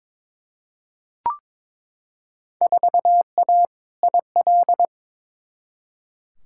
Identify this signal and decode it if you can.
Morse code, heard as a keyed tone, about 22 wpm, 700 Hz. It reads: 4A IL
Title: Field Day Exchanges - 22wpm